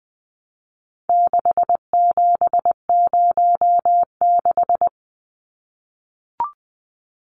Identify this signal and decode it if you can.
Morse code, heard as a keyed tone, about 20 wpm, 700 Hz. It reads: 6706
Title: Numbers - 4-Digit 20wpm